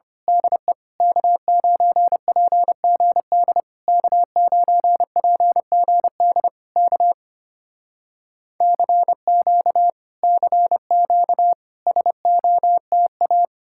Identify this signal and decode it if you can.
Morse code, heard as a keyed tone, about 25 wpm, 700 Hz. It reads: DE K9PGD K9PGD K CQ CQ SOTA
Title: Speed-Racing Common QSO Phrases Encore - 25wpm